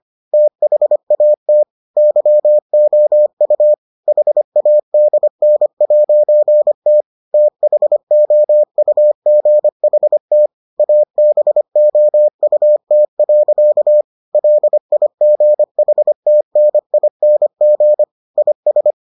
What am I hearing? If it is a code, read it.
Morse code, heard as a keyed tone, about 25 wpm, 600 Hz. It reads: THAT YOU HADN'T THOUGHT ABOUT. LIGHTNING IS